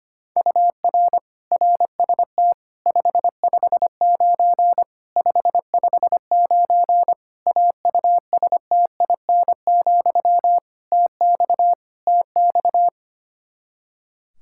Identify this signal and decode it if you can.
Morse code, heard as a keyed tone, about 25 wpm, 700 Hz. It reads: UR RST 559 559 AUSTIN, TX TX
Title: K1USN SST QSO Practice 25wpm